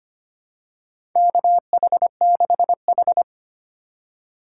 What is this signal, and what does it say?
Morse code, heard as a keyed tone, about 25 wpm, 700 Hz. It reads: KH6H
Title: USA Call Signs - 4 Characters - 4x Character Spacing - 25wpm